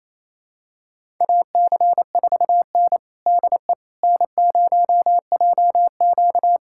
Morse code, heard as a keyed tone, about 28 wpm, 700 Hz. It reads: AC4N DE N0JQ
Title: USA Callsign DE Callsign - 4 Characters - 28wpm